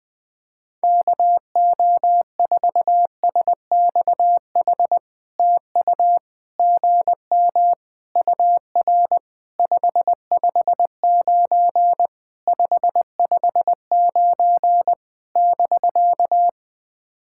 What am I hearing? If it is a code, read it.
Morse code, heard as a keyed tone, about 20 wpm, 700 Hz. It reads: KO4SXH TU GM UR 559 559 <BK>
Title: POTA QSO Practice - 20wpm